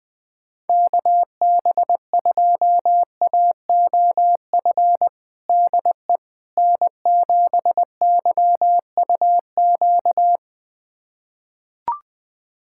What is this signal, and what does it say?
Morse code, heard as a keyed tone, about 21 wpm, 700 Hz. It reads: KB2AOF DE N7YUQ
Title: USA Callsign DE Callsign 20wpm